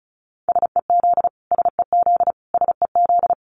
Morse code, heard as a keyed tone, about 34 wpm, 700 Hz. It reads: SEZ SEZ SEZ
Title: Abbreviations Repeated 3x - 35wpm